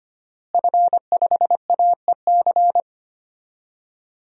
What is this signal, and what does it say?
Morse code, heard as a keyed tone, about 25 wpm, 700 Hz. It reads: F5AEC
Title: DXCC-100 Call Signs - 5 Characters - Repeated 3x - 25wpm